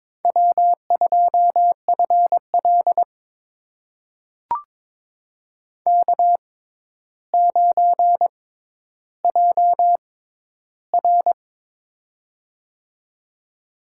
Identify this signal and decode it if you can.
Morse code, heard as a keyed tone, about 22 wpm, spaced as for 11 wpm, 700 Hz. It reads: W2FL K9JR
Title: USA Call Signs - 4 Characters - 6x Character Spacing - 22wpm